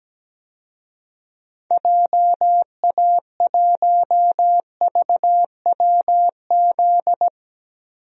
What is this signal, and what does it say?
Morse code, heard as a keyed tone, about 17 wpm, 700 Hz. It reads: JA1VWZ
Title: DXCC-100 Call Signs - 6 Characters - Repeated 3x - 17wpm